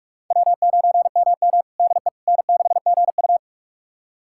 Morse code, heard as a keyed tone, about 44 wpm, 690 Hz. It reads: W9MM DE N6GU
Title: USA Callsign DE Callsign - 4 Characters - 45wpm